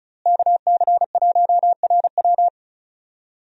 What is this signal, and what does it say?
Morse code, heard as a keyed tone, about 36 wpm, 700 Hz. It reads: KC1RW
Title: USA Call Signs - 5 Characters - Repeated 2x - 35wpm